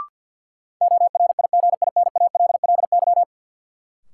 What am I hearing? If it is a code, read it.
Morse code, heard as a keyed tone, about 53 wpm, 700 Hz. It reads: ORIGINALLY
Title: Sentences from Top 1500 Words - New Word Drill Alt - 50wpm